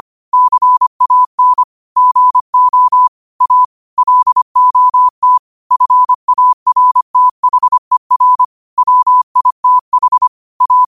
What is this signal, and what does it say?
Morse code, heard as a keyed tone, about 25 wpm, 1000 Hz. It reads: CAN GO A LOT FARTHER WITH A
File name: qotdfiles / 25wpm